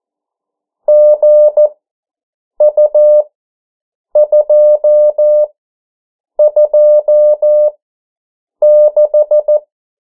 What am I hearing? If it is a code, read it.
Morse code, heard as a keyed tone, about 14 wpm, spaced as for 7 wpm, 600 Hz. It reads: GU226
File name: morse mp3 / 09 urok 8